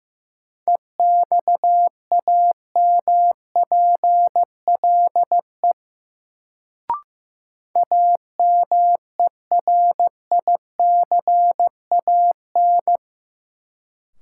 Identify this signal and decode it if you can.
Morse code, heard as a keyed tone, about 15 wpm, 700 Hz. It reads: EXAMPLE AMERICAN
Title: Sets of 2 Words - New Word Drill - 15wpm